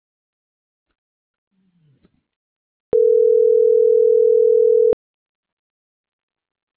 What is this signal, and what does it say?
Ringback tone (North America), 440 + 480 Hz.
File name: CENTRAL MAERSK / out luis.herrera